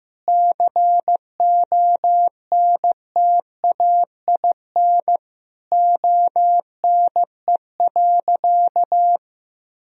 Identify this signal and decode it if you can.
Morse code, heard as a keyed tone, about 15 wpm, 700 Hz. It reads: CONTAIN ONE.
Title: Sentences from Top 1000 Words - Review Part 2 - 15wpm